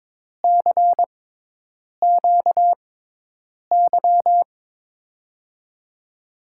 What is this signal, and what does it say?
Morse code, heard as a keyed tone, about 22 wpm, spaced as for 8 wpm, 700 Hz. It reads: CQY
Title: USA Call Signs - 6x Character Spacing - 22wpm